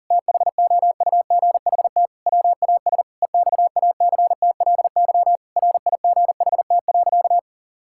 Morse code, heard as a keyed tone, about 37 wpm, 700 Hz. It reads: THOUGHT WAS EXACTLY RIGHT.
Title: Sentences from Top 1000 Words - Review Part 6 - 40wpm